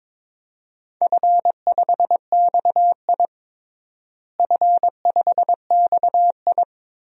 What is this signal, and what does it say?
Morse code, heard as a keyed tone, about 22 wpm, 700 Hz. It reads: F5XI F5XI
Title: DXCC-100 Call Signs - 4 Characters - Repeated 2x - 22wpm